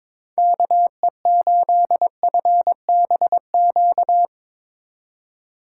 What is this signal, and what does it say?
Morse code, heard as a keyed tone, about 22 wpm, 700 Hz. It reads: KE8FBQ
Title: USA Call Signs - 6 Characters - 2x Character Spacing - 22wpm